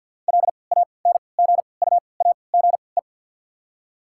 Morse code, heard as a keyed tone, about 48 wpm, spaced as for 28 wpm, 690 Hz. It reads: LANGUAGE
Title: 3x Character Spacing - Top 500 Words - 50wpm